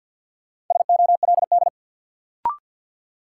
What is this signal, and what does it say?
Morse code, heard as a keyed tone, about 49 wpm, 700 Hz. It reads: SOLD
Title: Sentences from Top 1400 Words - New Word Drill - 50wpm